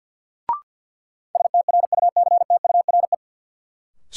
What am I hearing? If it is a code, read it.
Morse code, heard as a keyed tone, about 52 wpm, 700 Hz. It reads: STRUCTURE
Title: Sentences from Top 1500 Words - New Word Drill Alt - 50wpm